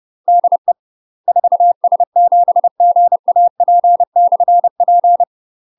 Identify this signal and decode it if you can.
Morse code, heard as a keyed tone, about 30 wpm, 700 Hz. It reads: DE 4S7GAP/P
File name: UK MASTER QSOs TRAINING / 30WPM / UK MASTER QSOs 0051